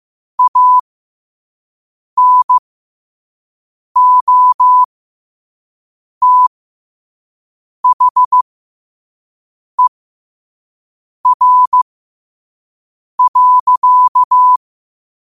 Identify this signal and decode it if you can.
Morse code, heard as a keyed tone, about 15 wpm, spaced as for 5 wpm, 1000 Hz. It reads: ANOTHER.
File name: qotdfiles / 5wpm